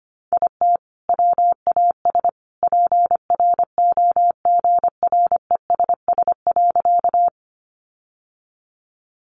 Morse code, heard as a keyed tone, about 25 wpm, 700 Hz. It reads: IT WAS PROGRESS.
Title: Sentences from Top 1400 Words 25wpm